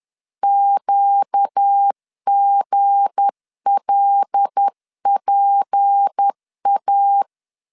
Morse code, heard as a keyed tone, about 10 wpm, 800 Hz. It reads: QGLPA
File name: LEZIONI / 7 GF9Z / 7 G 10wpm 2501